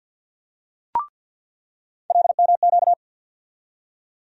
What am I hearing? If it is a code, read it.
Morse code, heard as a keyed tone, about 47 wpm, 700 Hz. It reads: LMQ
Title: USA Call Sign Postfixes - 3 Letters 50wpm